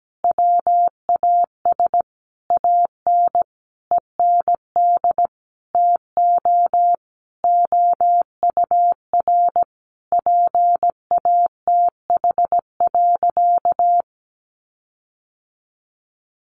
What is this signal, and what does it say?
Morse code, heard as a keyed tone, about 17 wpm, 700 Hz. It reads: WAS AN END TO OUR PATH.
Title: Sentences from Top 1400 Words 17wpm